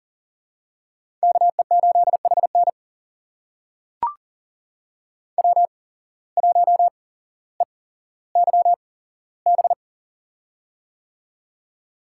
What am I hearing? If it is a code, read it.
Morse code, heard as a keyed tone, about 40 wpm, spaced as for 17 wpm, 700 Hz. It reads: KE8HN W1EYB
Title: USA Call Signs - 5 Characters - 8x Character Spacing - 40wpm